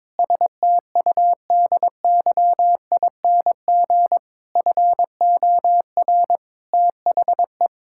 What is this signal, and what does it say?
Morse code, heard as a keyed tone, about 22 wpm, 700 Hz. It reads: STUDYING FOR THE
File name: Sentences with 7-Word Phrases - 22wpm